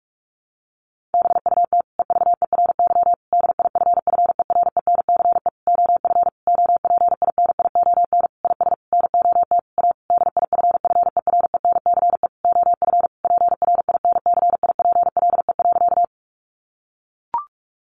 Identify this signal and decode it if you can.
Morse code, heard as a keyed tone, about 44 wpm, 700 Hz. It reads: BUT EVERY DIFFERENCE OF OPINION IS NOT A DIFFERENCE OF PRINCIPLE.
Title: Sentences from Top 1400 Words 45wpm